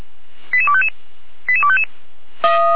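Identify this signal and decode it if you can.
Five-tone selective call (ZVEI), heard as quick runs of five tones, tones of about 70 ms. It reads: 88255 88255
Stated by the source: ZVEI five-tone selective call received on an SDR, hosted by feuerwehr-orlamuende (Orlamünde fire brigade).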